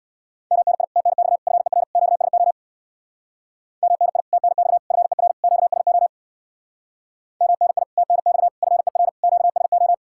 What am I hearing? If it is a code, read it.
Morse code, heard as a keyed tone, about 60 wpm, 700 Hz. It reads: MNI TNX FER QSO MNI TNX FER QSO MNI TNX FER QSO
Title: Speed-Racing Common QSO Phrases Encore - 50wpm